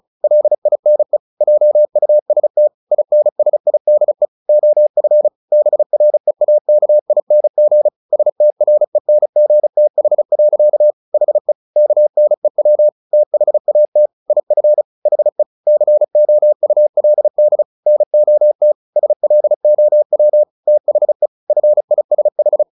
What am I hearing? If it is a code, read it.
Morse code, heard as a keyed tone, about 34 wpm, 600 Hz. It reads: LINE JUST INSIDE OF BREAKING STRENGTH. HE KNEW THAT IF HE COULD NOT SLOW THE FISH
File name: the old man and the sea / the old man and the sea 1 35wpm